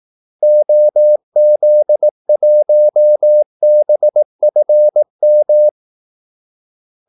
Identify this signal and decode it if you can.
Morse code, heard as a keyed tone, about 18 wpm, 600 Hz. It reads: OZ1BFM
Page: Kaldesignaler 18 wpm